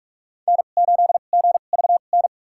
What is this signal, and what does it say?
Morse code, heard as a keyed tone, about 44 wpm, spaced as for 32 wpm, 700 Hz. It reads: N8GVN
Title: USA Call Signs - 5 Characters - 2x Character Spacing - 45wpm